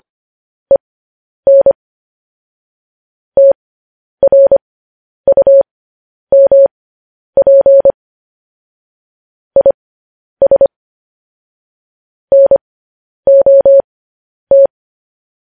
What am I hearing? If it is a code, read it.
Morse code, heard as a keyed tone, about 25 wpm, spaced as for 10 wpm, 550 Hz. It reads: EN TRUMP IS NOT
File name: podcast / News headlines-1x-25x10